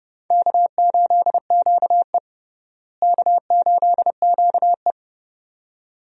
Morse code, heard as a keyed tone, about 32 wpm, 700 Hz. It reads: K8QE K8QE
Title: USA Call Signs - 4 Characters - Repeated 2x - 30wpm